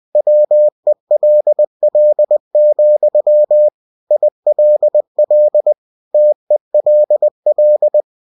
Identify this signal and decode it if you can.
Morse code, heard as a keyed tone, about 20 wpm, 600 Hz. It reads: WELL, ILL TELL
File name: cw-books / winnie-the-pooh / winnie-20wpm-page-0052